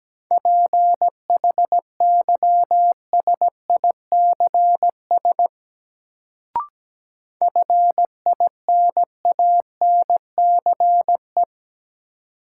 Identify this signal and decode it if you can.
Morse code, heard as a keyed tone, about 17 wpm, 700 Hz. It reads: PHYSICS FINANCE
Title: Sentences from Top 1500 Words - New Word Drill Alt - 17wpm